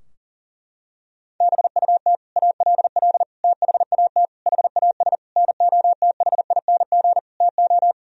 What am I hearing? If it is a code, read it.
Morse code, heard as a keyed tone, about 37 wpm, 700 Hz. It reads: BUT ALL THAT HAS NOTHING TO
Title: Sentences from Top 1000 Words - Review Part 3 - 40wpm